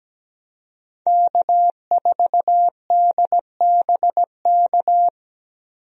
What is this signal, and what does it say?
Morse code, heard as a keyed tone, about 17 wpm, 700 Hz. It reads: K4DBK
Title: USA Call Signs - 5 Characters - Repeated 2x - 17wpm